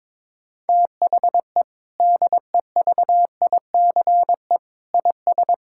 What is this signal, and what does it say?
Morse code, heard as a keyed tone, about 22 wpm, 700 Hz. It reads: THE DEVICE IS
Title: Sentences from Top 1600 Words 22wpm